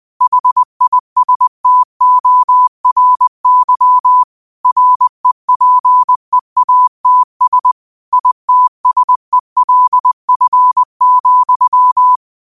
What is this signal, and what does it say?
Morse code, heard as a keyed tone, about 20 wpm, 1000 Hz. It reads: HISTORY REPEATS ITSELF,
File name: qotdfiles / 20wpm